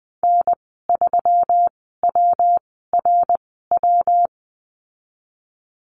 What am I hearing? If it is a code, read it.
Morse code, heard as a keyed tone, about 20 wpm, 700 Hz. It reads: N 3 W R W
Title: 154 Lesson 25g - Letter p - Call Signs 20wpm